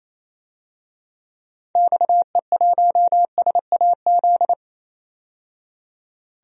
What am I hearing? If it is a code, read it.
Morse code, heard as a keyed tone, about 28 wpm, 700 Hz. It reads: XE1SAZ
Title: DXCC-100 Call Signs - 6 Characters - 28wpm